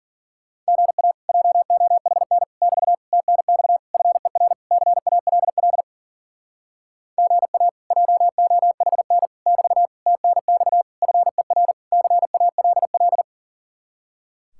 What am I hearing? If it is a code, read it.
Morse code, heard as a keyed tone, about 50 wpm, 700 Hz. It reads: GA JOHN <BT> TNX FER CALL GA JOHN <BT> TNX FER CALL